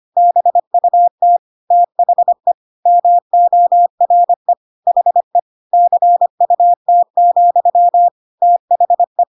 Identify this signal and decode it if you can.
Morse code, heard as a keyed tone, about 25 wpm, 700 Hz. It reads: BUT THE MORE HE CUT, THE
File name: Pinocchio / Pinocchio ch3 25 25 ik1hge